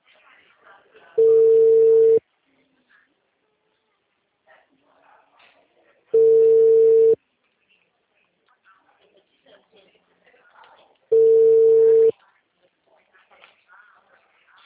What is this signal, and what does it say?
Ringback tone (China, 450 Hz), with speech.